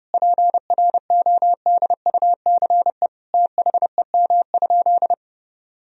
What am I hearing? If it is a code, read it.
Morse code, heard as a keyed tone, about 30 wpm, 700 Hz. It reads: PRODUCE THEM?
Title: Sentences from Top 1000 Words - Review Part 3 - 30wpm